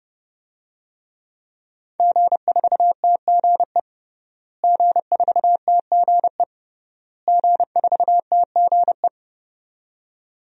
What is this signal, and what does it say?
Morse code, heard as a keyed tone, about 30 wpm, 700 Hz. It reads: G4TGE G4TGE G4TGE
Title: DXCC-100 Call Signs - 5 Characters - Repeated 3x - 30wpm